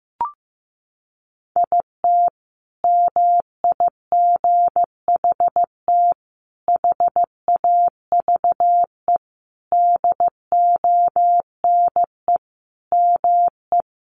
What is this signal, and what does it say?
Morse code, heard as a keyed tone, about 15 wpm, 700 Hz. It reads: IT MIGHT HAVE DONE ME
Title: Sentences from Top 500 Words 15wpm